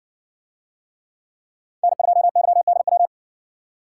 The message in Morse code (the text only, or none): N2YDW